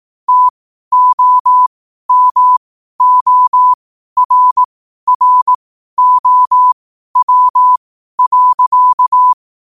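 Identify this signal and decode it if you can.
Morse code, heard as a keyed tone, about 18 wpm, spaced as for 12 wpm, 1000 Hz. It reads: TOMORROW.
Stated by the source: Morse code sent at 12 words per minute.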